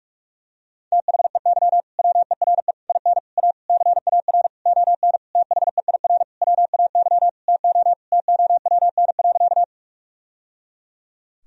Morse code, heard as a keyed tone, about 46 wpm, 700 Hz. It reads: THEY WERE IN A CAR ON THEIR WAY TO TOWN.
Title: Sentences from Top 1000 Words - Review Part 1 - 45wpm